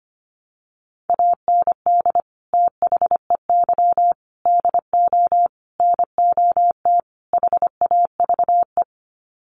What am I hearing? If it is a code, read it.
Morse code, heard as a keyed tone, about 25 wpm, 700 Hz. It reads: AND THEY DO NOT HAVE
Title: Sets of 5 Words Encore2 25wpm